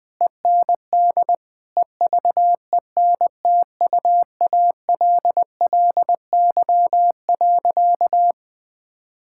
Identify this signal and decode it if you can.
Morse code, heard as a keyed tone, about 21 wpm, 700 Hz. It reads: END EVENTUALLY.
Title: Sentences with 8-Word Phrases - Spoken Beginning - 20wpm